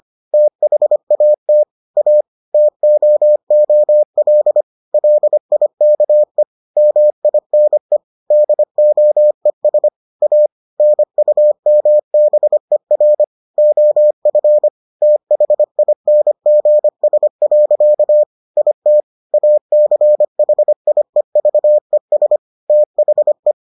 Morse code, heard as a keyed tone, about 25 wpm, 600 Hz. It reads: THAT A TOOL LIKE MINE DOES A NUMBER OF THINGS. IT ACHIEVES THE